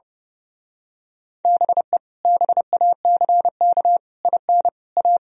Morse code, heard as a keyed tone, about 31 wpm, 700 Hz. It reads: BE BACK IN A
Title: Speed-Racing - Sets of 5 Words 30wpm